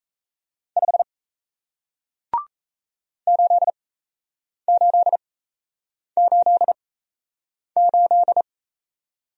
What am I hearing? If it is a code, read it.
Morse code, heard as a keyed tone, about 38 wpm, spaced as for 8 wpm, 700 Hz. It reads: 5 8888